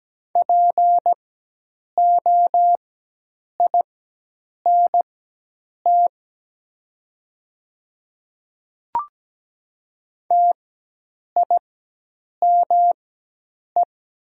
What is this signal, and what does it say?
Morse code, heard as a keyed tone, about 17 wpm, spaced as for 8 wpm, 700 Hz. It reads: POINT TIME
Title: 4x Character Spacing - Top 200 Words - 17wpm